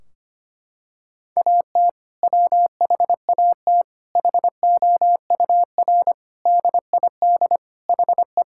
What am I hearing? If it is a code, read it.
Morse code, heard as a keyed tone, about 25 wpm, 700 Hz. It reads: AT WHAT HOUR DID HE
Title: Sentences from Top 1600 Words 25wpm